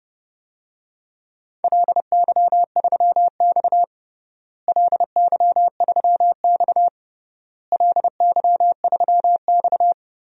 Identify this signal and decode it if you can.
Morse code, heard as a keyed tone, about 30 wpm, 700 Hz. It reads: LY3X LY3X LY3X
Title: DXCC-100 Call Signs - 4 Characters - Repeated 3x - 30wpm